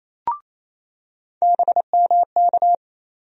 Morse code, heard as a keyed tone, about 28 wpm, 700 Hz. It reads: BMK
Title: USA Call Sign Postfixes - 3 Letters 28wpm